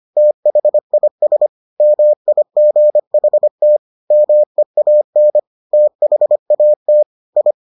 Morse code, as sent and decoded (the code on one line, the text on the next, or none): - .... .. ... / -- .. --. .... - / -- . .- -. / - .... .- - / ..
THIS MIGHT MEAN THAT I